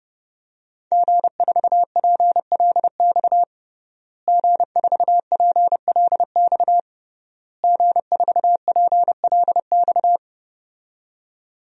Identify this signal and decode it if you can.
Morse code, heard as a keyed tone, about 32 wpm, 700 Hz. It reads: G4PLX G4PLX G4PLX
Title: DXCC-100 Call Signs - 5 Characters - Repeated 3x - 30wpm